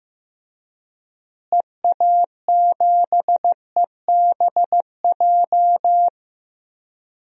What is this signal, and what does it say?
Morse code, heard as a keyed tone, about 15 wpm, 700 Hz. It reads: EA7EBJ
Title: DXCC-100 Call Signs - 6 Characters - Repeated 2x - 15wpm